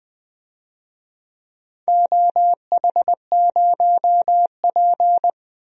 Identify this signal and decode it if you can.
Morse code, heard as a keyed tone, about 20 wpm, 700 Hz. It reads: OH0P
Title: DXCC-100 Call Signs - 4 Characters - 20wpm